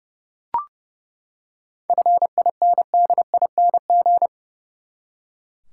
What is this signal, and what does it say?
Morse code, heard as a keyed tone, about 29 wpm, 700 Hz. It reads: FINDING
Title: Seven Letter Words - 30wpm